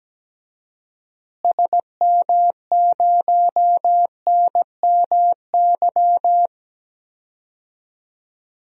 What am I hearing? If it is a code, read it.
Morse code, heard as a keyed tone, about 17 wpm, 700 Hz. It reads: SM0NMY
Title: DXCC-100 Call Signs - 6 Characters - Repeated 2x - 17wpm